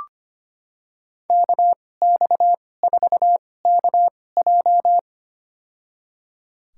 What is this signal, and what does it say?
Morse code, heard as a keyed tone, about 25 wpm, spaced as for 18 wpm, 700 Hz. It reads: KX4KJ
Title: USA Call Signs - 5 Characters - 2x Character Spacing - 25wpm